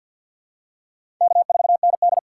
Morse code, heard as a keyed tone, about 50 wpm, 700 Hz. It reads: K4ND
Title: USA Call Signs - 4 Characters - 3x Character Spacing - 50wpm